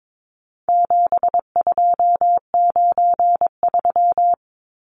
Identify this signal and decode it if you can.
Morse code, heard as a keyed tone, about 22 wpm, 700 Hz. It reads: 7293
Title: Numbers - 4-Digit 22wpm